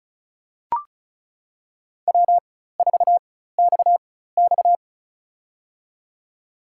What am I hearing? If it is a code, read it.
Morse code, heard as a keyed tone, about 36 wpm, spaced as for 16 wpm, 700 Hz. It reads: W4XX
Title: USA Call Signs - 4 Characters - 4x Character Spacing - 35wpm